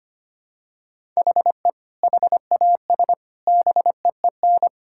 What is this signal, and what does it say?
Morse code, heard as a keyed tone, about 25 wpm, 700 Hz. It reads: HE HAS BEEN
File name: practice / 25wpm / Sentences with 4-Word Phrases - 25wpm